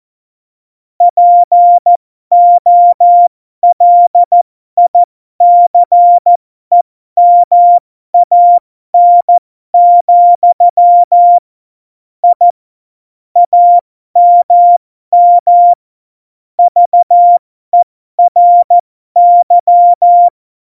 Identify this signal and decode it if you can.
Morse code, heard as a keyed tone, about 14 wpm, spaced as for 12 wpm, 700 Hz. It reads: POLICEMAN, I AMM VERY
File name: Morsekurs / Witze